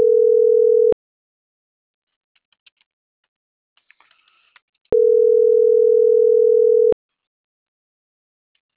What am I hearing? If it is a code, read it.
Ringback tone (North America), 440 + 480 Hz.
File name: CENTRAL SAGOT / out rafael.zamorategui